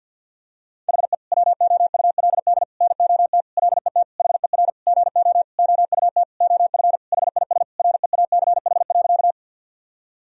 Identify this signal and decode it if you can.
Morse code, heard as a keyed tone, about 49 wpm, 700 Hz. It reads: HE WOULD NOT LET HER GO OUT OF HIS REACH.
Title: Sentences from Top 1000 Words - Review Part 2 - 50wpm